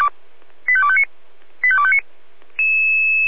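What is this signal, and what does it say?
Five-tone selective call (ZVEI), heard as quick runs of five tones, tones of about 70 ms. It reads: 85259 85259
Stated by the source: Five-tone ZVEI selective call received by SDR, from the audio archive of feuerwehr-orlamuende.org fire brigade.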